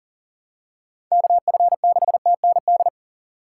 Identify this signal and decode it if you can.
Morse code, heard as a keyed tone, about 40 wpm, 700 Hz. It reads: KF6TND
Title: USA Call Signs - 6 Characters - 4x Character Spacing - 40wpm